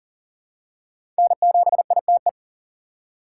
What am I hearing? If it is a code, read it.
Morse code, heard as a keyed tone, about 40 wpm, 700 Hz. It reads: N7ITE
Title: USA Call Signs - 5 Characters - 4x Character Spacing - 40wpm